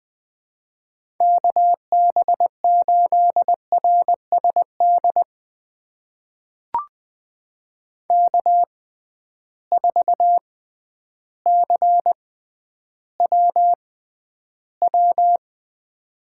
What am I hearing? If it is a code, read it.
Morse code, heard as a keyed tone, about 21 wpm, spaced as for 11 wpm, 700 Hz. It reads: KB8RSD K4CWW
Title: USA Call Signs - 6x Character Spacing - 20wpm